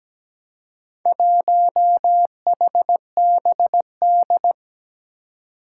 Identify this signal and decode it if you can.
Morse code, heard as a keyed tone, about 17 wpm, 700 Hz. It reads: 1HBD
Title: Four Challenging Letter-Number Combinations 17wpm